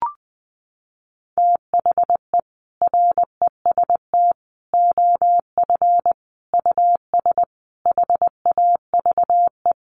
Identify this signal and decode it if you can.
Morse code, heard as a keyed tone, about 20 wpm, 700 Hz. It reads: THE REST OF US HAVE
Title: Sentences from Top 700 Words 20wpm